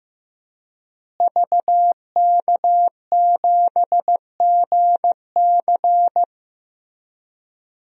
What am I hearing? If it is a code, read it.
Morse code, heard as a keyed tone, about 15 wpm, 700 Hz. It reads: VK7GC
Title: DXCC-100 Call Signs - 5 Characters - Repeated 3x - 15wpm